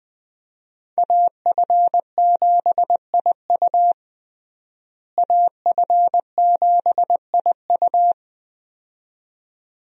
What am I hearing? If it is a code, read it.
Morse code, heard as a keyed tone, about 20 wpm, 700 Hz. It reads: AF7IU AF7IU
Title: USA Call Signs - 5 Characters - Repeated 2x - 20wpm